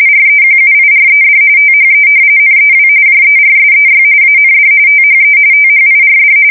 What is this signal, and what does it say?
RTTY, about 45.45 baud. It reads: REQUIRED THE NEW JERSEY AND THE THIRD
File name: fsk170 / battleshipnj